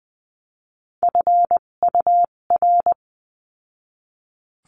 Morse code, heard as a keyed tone, about 20 wpm, spaced as for 17 wpm, 700 Hz. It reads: FUR
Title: 257 Lesson 42b - Words 14-20wpm